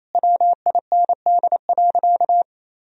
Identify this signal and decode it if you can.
Morse code, heard as a keyed tone, about 27 wpm, 700 Hz. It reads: WIND.
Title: Sentences from Top 1000 Words - Review Part 2 - 28wpm